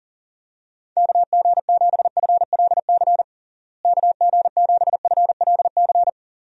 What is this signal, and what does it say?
Morse code, heard as a keyed tone, about 40 wpm, 700 Hz. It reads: KG7FLC KG7FLC
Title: USA Call Signs - 6 Characters - Repeated 2x - 40wpm